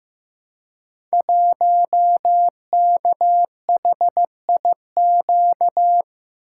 Morse code, heard as a keyed tone, about 15 wpm, 700 Hz. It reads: 1KHIQ
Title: Code Groups - 5 Letter:Numbers - 15wpm